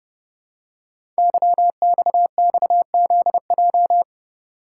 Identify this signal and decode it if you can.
Morse code, heard as a keyed tone, about 30 wpm, 700 Hz. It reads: YXXZJ
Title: Code Groups - 5 Letters - 30wpm